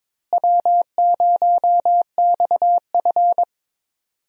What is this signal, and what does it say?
Morse code, heard as a keyed tone, about 22 wpm, 700 Hz. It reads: W0XF
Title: USA Call Signs - 4 Characters - Repeated 3x - 22wpm